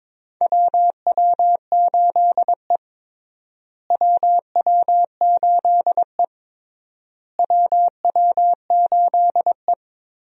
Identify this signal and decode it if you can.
Morse code, heard as a keyed tone, about 21 wpm, 700 Hz. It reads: WW8E WW8E WW8E
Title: USA Call Signs - 4 Characters - Repeated 3x - 22wpm